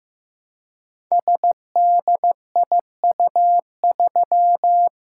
Morse code, heard as a keyed tone, about 15 wpm, 700 Hz. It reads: SDIU3
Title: Code Groups - 5 Letter:Numbers - 15wpm